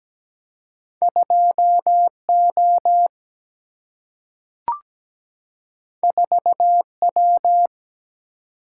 Morse code, heard as a keyed tone, about 17 wpm, 700 Hz. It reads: 2 O 4 W
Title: Single-Letter Single-Number 17wpm